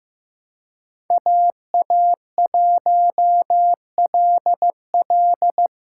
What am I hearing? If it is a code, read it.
Morse code, heard as a keyed tone, about 15 wpm, 700 Hz. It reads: AA1LL
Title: USA Call Signs - 5 Characters - 3x Character Spacing - 15wpm